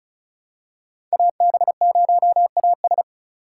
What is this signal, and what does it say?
Morse code, heard as a keyed tone, about 36 wpm, 700 Hz. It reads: AB0AS
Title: USA Call Signs - 5 Characters - 3x Character Spacing - 35wpm